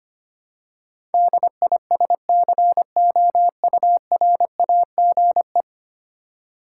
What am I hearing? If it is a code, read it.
Morse code, heard as a keyed tone, about 25 wpm, 700 Hz. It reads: DISCOURAGE
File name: Ten Letter Words - 25wpm